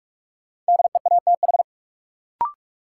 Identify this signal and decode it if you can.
Morse code, heard as a keyed tone, about 46 wpm, 700 Hz. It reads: DEATH
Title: Sentences from Top 1100 Words - New Word Drill Alt - 45wpm